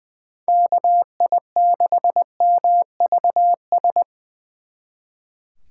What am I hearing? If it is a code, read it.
Morse code, heard as a keyed tone, about 24 wpm, 700 Hz. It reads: KI6MVS
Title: Speed Racing - USA 2x3 Call Signs 20wpm